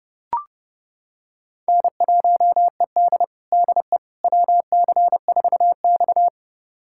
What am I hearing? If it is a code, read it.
Morse code, heard as a keyed tone, about 30 wpm, 700 Hz. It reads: N1ED DE WC4X
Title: USA Callsign DE Callsign - 4 Characters - 30wpm